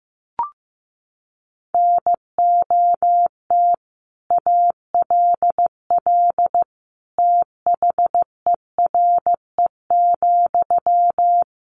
Morse code, heard as a keyed tone, about 15 wpm, 700 Hz. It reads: NOT ALL THERE,
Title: Sentences from Top 600 Words 15wpm